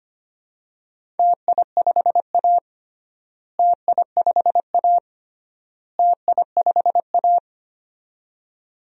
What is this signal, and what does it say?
Morse code, heard as a keyed tone, about 25 wpm, 700 Hz. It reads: TI5A TI5A TI5A
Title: DXCC-100 Call Signs - 4 Characters - Repeated 3x - 25wpm